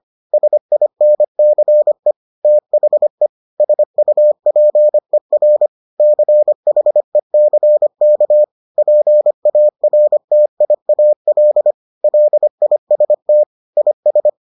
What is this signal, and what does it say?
Morse code, heard as a keyed tone, about 25 wpm, 600 Hz. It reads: SINCE THE SUPER CHECK PARTIAL LIST IS